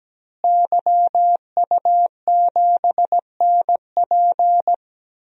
Morse code, heard as a keyed tone, about 17 wpm, 700 Hz. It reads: YU7NP
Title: DXCC-100 Call Signs - 5 Characters - Repeated 2x - 17wpm